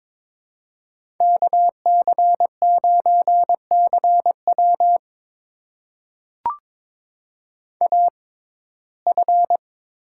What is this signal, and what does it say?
Morse code, heard as a keyed tone, about 22 wpm, spaced as for 16 wpm, 700 Hz. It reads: KC9CW AF
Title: USA Call Signs - 5 Characters - 6x Character Spacing - 22wpm